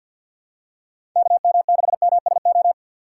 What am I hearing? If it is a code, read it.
Morse code, heard as a keyed tone, about 50 wpm, 700 Hz. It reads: KM6MSO
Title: USA Call Signs - 4x Character Spacing - 50wpm